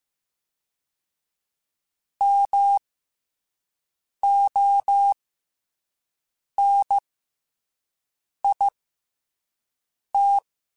Morse code, heard as a keyed tone, about 15 wpm, spaced as for 5 wpm, 800 Hz. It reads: MONIT
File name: Special INGs-mp3s / Special INGs-mp3s / SlowToFaster ing @10